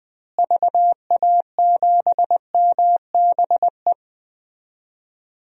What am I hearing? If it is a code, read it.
Morse code, heard as a keyed tone, about 20 wpm, 700 Hz. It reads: VA7MBE
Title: DXCC-100 Call Signs - 6 Characters - Repeated 2x - 20wpm